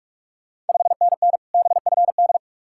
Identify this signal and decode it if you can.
Morse code, heard as a keyed tone, about 46 wpm, 700 Hz. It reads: 5NN BFD
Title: Ontario QSO Party Exchange 45wpm